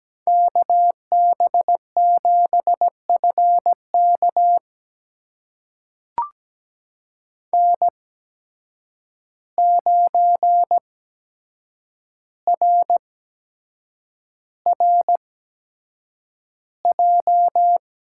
Morse code, heard as a keyed tone, about 17 wpm, spaced as for 7 wpm, 700 Hz. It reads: KB7FK N9RRJ
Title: USA Call Signs - 5 Characters - 8x Character Spacing - 17wpm